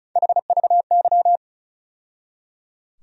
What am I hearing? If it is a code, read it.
Morse code, heard as a keyed tone, about 36 wpm, 700 Hz. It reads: HVY
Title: USA Call Sign Postfixes - 3 Letters 35wpm